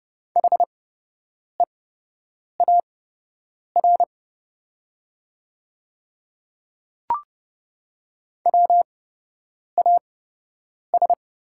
Morse code, heard as a keyed tone, about 30 wpm, spaced as for 8 wpm, 700 Hz. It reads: HEAR WAS
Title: 8x Character Spacing - Top 300 Words - 30wpm